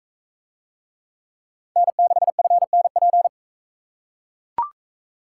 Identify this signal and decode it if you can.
Morse code, heard as a keyed tone, about 41 wpm, 700 Hz. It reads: N6FNP